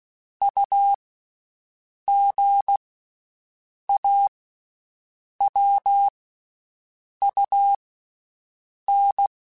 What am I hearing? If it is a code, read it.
Morse code, heard as a keyed tone, about 16 wpm, 800 Hz. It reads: U G A W U N
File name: Basic Sets / Basic Letter Groups Collective7 / AWUNDG@13